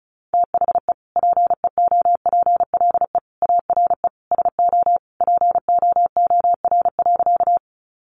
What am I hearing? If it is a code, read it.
Morse code, heard as a keyed tone, about 34 wpm, 700 Hz. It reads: THE PEOPLE ARE SO POOR.
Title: Sentences from Top 1000 Words 35wpm